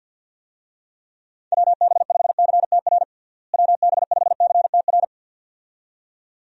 Morse code, heard as a keyed tone, about 53 wpm, 700 Hz. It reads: WB5CTR WB5CTR
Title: USA Call Signs - 6 Characters - Repeated 2x - 50wpm